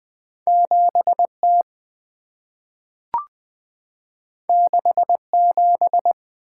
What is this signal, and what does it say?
Morse code, heard as a keyed tone, about 20 wpm, 690 Hz. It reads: 7T 67
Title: Two Letter-Number Combinations - 20wpm